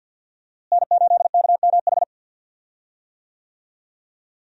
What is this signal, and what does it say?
Morse code, heard as a keyed tone, about 51 wpm, 700 Hz. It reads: N8KMH
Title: USA Call Signs - 5 Characters - Repeated 3x - 50wpm